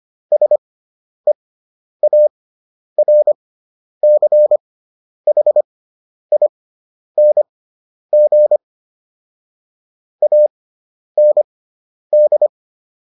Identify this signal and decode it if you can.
Morse code, heard as a keyed tone, about 25 wpm, spaced as for 10 wpm, 600 Hz. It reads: SEARCHING AND